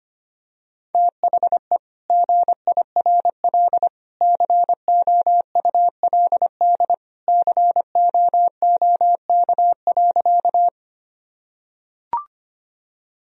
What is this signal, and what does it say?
Morse code, heard as a keyed tone, about 25 wpm, 700 Hz. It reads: THE GIRL COULD COOK.
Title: Sentences from Top 1000 Words - Review Part 4 - 25wpm